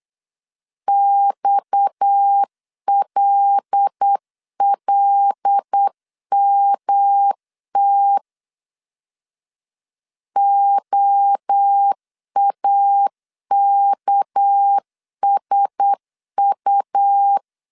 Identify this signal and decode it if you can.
Morse code, heard as a keyed tone, about 8 wpm, 800 Hz. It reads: XLLMT OAKSU
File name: CW / DET / 1 6 1500C S A 8wpm 5203